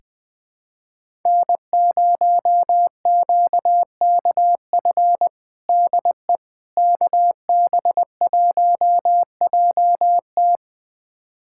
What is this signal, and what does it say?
Morse code, heard as a keyed tone, about 20 wpm, 700 Hz. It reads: N0QKF DE KB1JT
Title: USA Callsign DE Callsign - 5 Characters - 20wpm